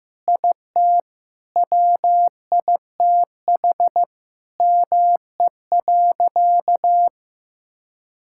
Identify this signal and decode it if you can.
Morse code, heard as a keyed tone, about 16 wpm, 700 Hz. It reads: IT WITH ME.
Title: Sentences from Top 1000 Words - Review Part 6 - 15wpm